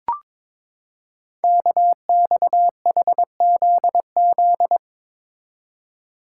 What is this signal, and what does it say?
Morse code, heard as a keyed tone, about 22 wpm, 700 Hz. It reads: KXHZZ
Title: Code Groups - 5 Letters - 22wpm